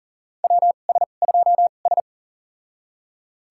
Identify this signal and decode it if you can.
Morse code, heard as a keyed tone, about 40 wpm, spaced as for 29 wpm, 700 Hz. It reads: WS2S
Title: USA Call Signs - 4 Characters - 2x Character Spacing - 40wpm